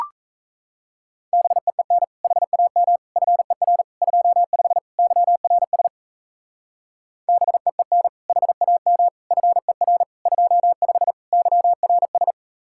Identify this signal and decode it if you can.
Morse code, heard as a keyed tone, about 34 wpm, 700 Hz. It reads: BEEN HAM FER 25 YRS BEEN HAM FER 25 YRS
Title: Speed-Racing Common QSO Phrases Encore - 28wpm